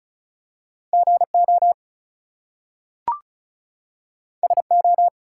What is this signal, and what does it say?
Morse code, heard as a keyed tone, about 36 wpm, 700 Hz. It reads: GO SO
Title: Two Letter Words - 35wpm